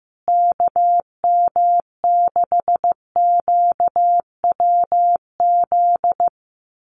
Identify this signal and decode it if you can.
Morse code, heard as a keyed tone, about 15 wpm, 700 Hz. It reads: KM6QWZ
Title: USA Call Signs Repeated 2x 15wpm